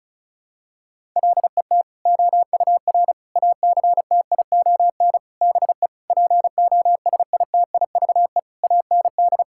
Morse code, heard as a keyed tone, about 36 wpm, 700 Hz. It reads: LET OUR ACTION BE POSITIVE AND
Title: Sentences from Top 1000 Words - Review Part 6 - 35wpm